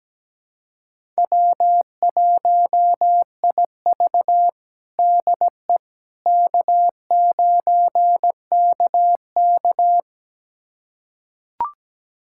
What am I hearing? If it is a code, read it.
Morse code, heard as a keyed tone, about 17 wpm, 700 Hz. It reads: W1IV DE K9KK
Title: USA Callsign DE Callsign - 4 Characters - 17wpm